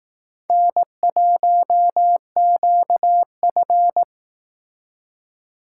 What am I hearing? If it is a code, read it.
Morse code, heard as a keyed tone, about 19 wpm, 700 Hz. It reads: N1QF